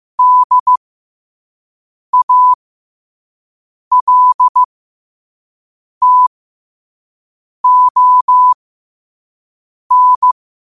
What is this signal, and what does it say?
Morse code, heard as a keyed tone, about 15 wpm, spaced as for 5 wpm, 1000 Hz. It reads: DALTON
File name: 5wpm